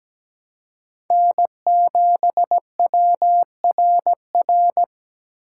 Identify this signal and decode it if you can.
Morse code, heard as a keyed tone, about 17 wpm, 700 Hz. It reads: N7WRR
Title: USA Call Signs - 5 Characters - 3x Character Spacing - 17wpm